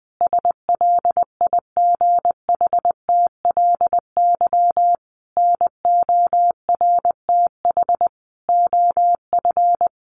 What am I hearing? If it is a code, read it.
Morse code, heard as a keyed tone, about 20 wpm, 700 Hz. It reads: SLIGHTLY NORTH OF
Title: Sentences from Top 1300 Words 20wpm